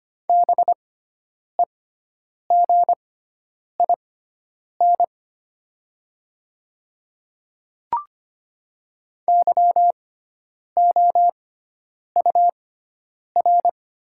Morse code, heard as a keyed tone, about 25 wpm, spaced as for 9 wpm, 700 Hz. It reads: BEGIN YOUR
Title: 6x Character Spacing - Top 300 Words - 25wpm